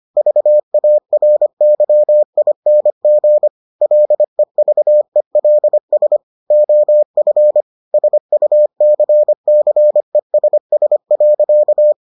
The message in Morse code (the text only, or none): VARYING LEVELS OF SUCCESS.